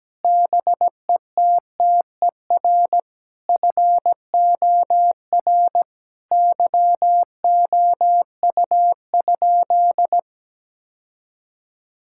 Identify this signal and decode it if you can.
Morse code, heard as a keyed tone, about 17 wpm, 700 Hz. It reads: BETTER FOR YOU?
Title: Sentences from Top 1000 Words - Review Part 5 - 17wpm